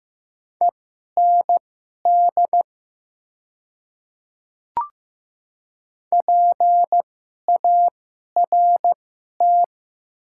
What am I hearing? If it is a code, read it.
Morse code, heard as a keyed tone, about 15 wpm, spaced as for 11 wpm, 700 Hz. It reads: END PART
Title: 2x Character Spacing - Top 200 Words - 15wpm